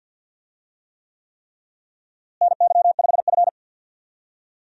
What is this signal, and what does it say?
Morse code, heard as a keyed tone, about 52 wpm, 700 Hz. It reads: NY5F
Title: USA Call Signs - 4 Characters - Repeated 3x - 50wpm